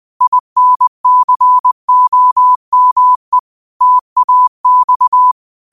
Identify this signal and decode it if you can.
Morse code, heard as a keyed tone, about 20 wpm, 1000 Hz. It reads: INCOME TAX